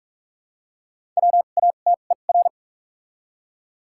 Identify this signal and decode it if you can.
Morse code, heard as a keyed tone, about 43 wpm, spaced as for 32 wpm, 700 Hz. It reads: WATER
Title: 2x Character Spacing - Top 100 Words - 45wpm